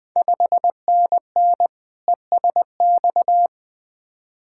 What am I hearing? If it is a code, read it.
Morse code, heard as a keyed tone, about 20 wpm, 700 Hz. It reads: 5NN ESX
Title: Ontario QSO Party Exchange 20wpm